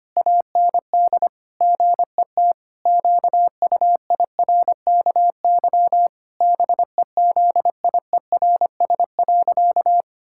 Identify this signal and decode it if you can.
Morse code, heard as a keyed tone, about 25 wpm, 700 Hz. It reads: AND GET QUIRKY BEZIERS.
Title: Pangram Challenge v2 - 25wpm